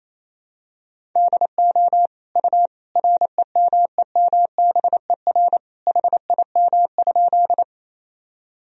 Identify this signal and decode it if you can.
Morse code, heard as a keyed tone, about 28 wpm, 700 Hz. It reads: DO U REMEMBER HIM?
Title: Sentences from Top 400 Words and Abbreviations 28wpm